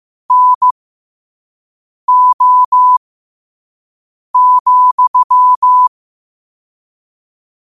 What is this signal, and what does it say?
Morse code, heard as a keyed tone, about 15 wpm, spaced as for 5 wpm, 1000 Hz. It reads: NO,
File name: qotdfiles / 5wpm